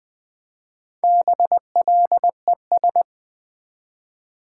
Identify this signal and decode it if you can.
Morse code, heard as a keyed tone, about 20 wpm, 700 Hz. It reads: BLES
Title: Build A Word - 1 Letter At A Time - 5-Letter Words - 20wpm